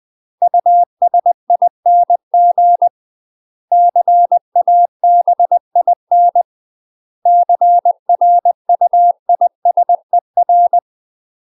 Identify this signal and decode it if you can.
Morse code, heard as a keyed tone, about 20 wpm, 700 Hz. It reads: USING CABIN CRUISER
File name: UK MASTER QSOs TRAINING / 20WPM / UK MASTER QSOs 0013